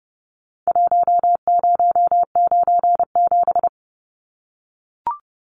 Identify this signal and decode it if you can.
Morse code, heard as a keyed tone, about 30 wpm, 700 Hz. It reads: 1097
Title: Numbers - Contest Style - 4-Digit 30wpm